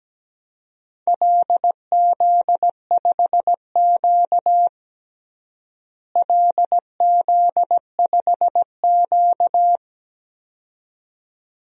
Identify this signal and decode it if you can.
Morse code, heard as a keyed tone, about 17 wpm, 700 Hz. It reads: LZ5Q LZ5Q
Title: DXCC-100 Call Signs - 4 Characters - Repeated 2x - 17wpm